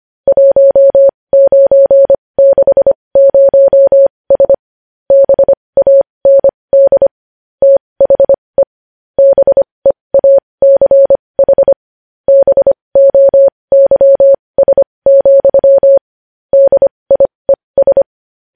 Morse code, heard as a keyed tone, about 25 wpm, spaced as for 20 wpm, 550 Hz. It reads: 1960S BAND THE BEACH BOYS, DIES
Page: News Headlines in Morse Code at 20 WPM